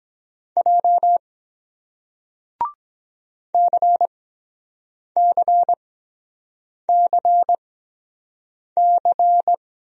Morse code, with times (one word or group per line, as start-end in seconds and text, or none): 0.6-1.2 J
3.5-9.5 CCCC